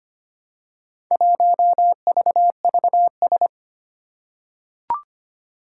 Morse code, heard as a keyed tone, about 25 wpm, 700 Hz. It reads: 1VVS
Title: Four Challenging Letter-Number Combinations 25wpm